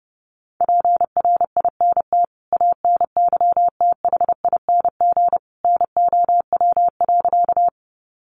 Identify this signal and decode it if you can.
Morse code, heard as a keyed tone, about 30 wpm, 700 Hz. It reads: PRINT ANYTHING NOW.
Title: Sentences from Top 1400 Words 30wpm